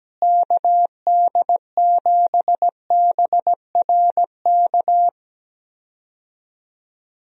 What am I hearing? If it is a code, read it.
Morse code, heard as a keyed tone, about 17 wpm, 700 Hz. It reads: KD7BRK
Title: USA Call Signs - 6 Characters - Repeated 3x - 17wpm